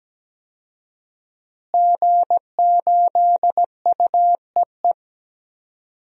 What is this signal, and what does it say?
Morse code, heard as a keyed tone, about 17 wpm, 700 Hz. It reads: G8UEE
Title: DXCC-100 Call Signs - 5 Characters - Repeated 3x - 17wpm